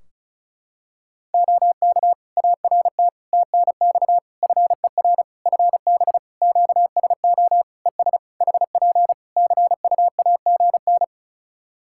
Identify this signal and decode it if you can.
Morse code, heard as a keyed tone, about 34 wpm, 700 Hz. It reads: OK ART TNX FER FB QSO ES HP CUAGN
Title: Practice QSO - Exchange 4 - Ending - 35wpm